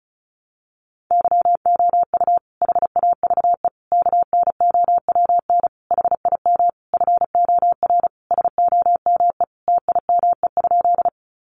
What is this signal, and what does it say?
Morse code, heard as a keyed tone, about 34 wpm, 700 Hz. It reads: YOU HAVE KNOWN HIM FOR SOME TIME?
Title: Sentences from Top 600 Words 35wpm